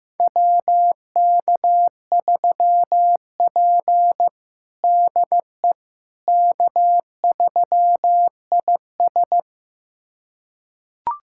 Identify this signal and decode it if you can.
Morse code, heard as a keyed tone, about 16 wpm, 700 Hz. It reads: WK3P DE K3IS
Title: USA Callsign DE Callsign - 4 Characters - 15wpm